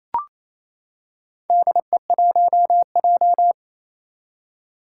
Morse code, heard as a keyed tone, about 28 wpm, 700 Hz. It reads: DE1J
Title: Four Challenging Letter-Number Combinations 28wpm